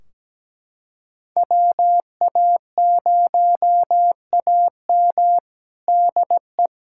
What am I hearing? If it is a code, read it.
Morse code, heard as a keyed tone, about 17 wpm, 700 Hz. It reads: WA0AM DE
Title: Practice QSO - Exchange 4 - Ending - 17wpm